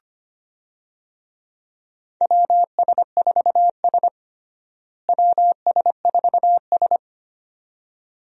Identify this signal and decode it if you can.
Morse code, heard as a keyed tone, about 25 wpm, 700 Hz. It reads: WS4S WS4S
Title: USA Call Signs - 4 Characters - Repeated 2x - 25wpm